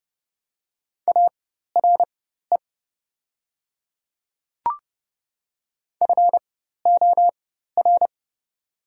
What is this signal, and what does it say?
Morse code, heard as a keyed tone, about 32 wpm, spaced as for 14 wpm, 690 Hz. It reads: ARE FOR
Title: 4x Character Spacing - Top 100 Words - 30wpm